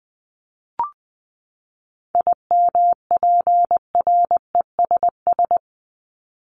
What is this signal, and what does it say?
Morse code, heard as a keyed tone, about 20 wpm, 700 Hz. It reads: IMPRESS
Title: Top 1501-1750 Words 20wpm